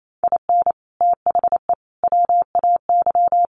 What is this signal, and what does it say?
Morse code, heard as a keyed tone, about 28 wpm, 700 Hz. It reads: IN THE WAY
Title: Sets of 3 Words 28wpm